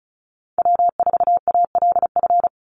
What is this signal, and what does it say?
Morse code, heard as a keyed tone, about 34 wpm, 700 Hz. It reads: W4ALF
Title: Call Signs - 35wpm